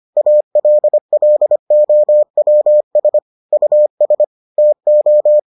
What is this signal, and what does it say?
Morse code, heard as a keyed tone, about 25 wpm, 600 Hz. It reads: ALLOWS US TO